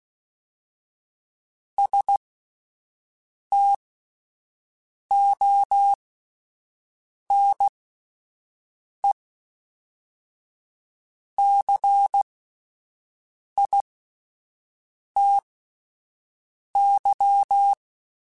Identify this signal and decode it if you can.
Morse code, heard as a keyed tone, about 16 wpm, spaced as for 5 wpm, 800 Hz. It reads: STONE CITY